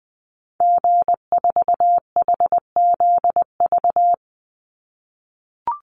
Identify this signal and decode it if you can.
Morse code, heard as a keyed tone, about 20 wpm, 700 Hz. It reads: G4HZV
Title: Call Signs - 20wpm